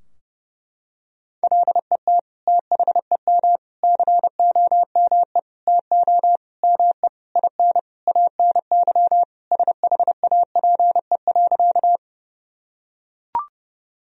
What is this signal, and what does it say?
Morse code, heard as a keyed tone, about 32 wpm, 700 Hz. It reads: LET THEM COME TO ME IN ANY SHAPE.
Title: Sentences from Top 1000 Words - Review Part 3 - 30wpm